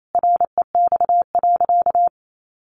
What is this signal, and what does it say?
Morse code, heard as a keyed tone, about 28 wpm, 700 Hz. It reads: REX.
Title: A Sound Of Thunder 28wpm